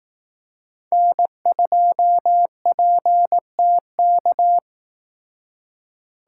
Morse code, heard as a keyed tone, about 18 wpm, 700 Hz. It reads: N2PTK
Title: Speed Racing - USA 1x3 Call Signs 15wpm